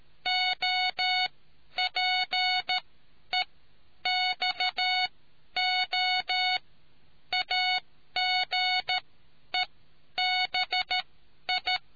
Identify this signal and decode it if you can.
Morse code, heard as a keyed tone, about 13 wpm, spaced as for 10 wpm, 740 Hz. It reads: OPEXO AGEBI